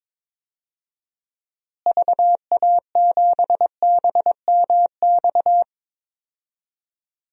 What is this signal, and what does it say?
Morse code, heard as a keyed tone, about 22 wpm, 700 Hz. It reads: VA7BMX
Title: DXCC-100 Call Signs - 6 Characters - Repeated 3x - 22wpm